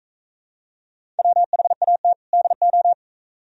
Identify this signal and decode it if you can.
Morse code, heard as a keyed tone, about 38 wpm, 700 Hz. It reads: WHAT DO